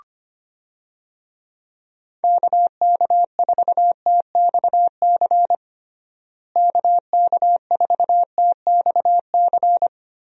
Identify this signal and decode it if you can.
Morse code, heard as a keyed tone, about 25 wpm, 700 Hz. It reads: KK4TXC KK4TXC
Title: USA Call Signs - 6 Characters - Repeated 3x - 25wpm